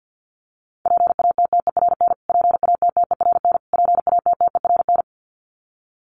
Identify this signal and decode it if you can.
Morse code, heard as a keyed tone, about 50 wpm, 700 Hz. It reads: PATTERN PATTERN PATTERN
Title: Top 500 Words Repeated 50wpm